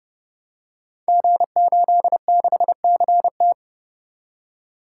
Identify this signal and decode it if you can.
Morse code, heard as a keyed tone, about 30 wpm, 700 Hz. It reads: G86CT
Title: Code Groups - 5 Letter:Numbers - 30wpm